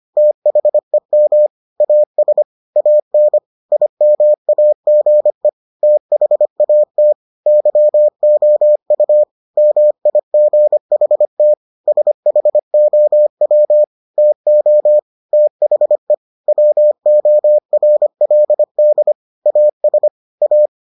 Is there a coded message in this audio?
Yes: Morse code from 0.2 to 20.7 s, heard as a keyed tone, about 25 wpm, 600 Hz. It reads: THEM AS AN IMAGE THAT YOU MIGHT SHOW TO THE WORLD AS A